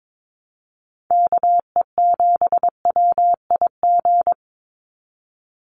Morse code, heard as a keyed tone, about 22 wpm, 700 Hz. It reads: KE7WIG
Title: USA Call Signs 22wpm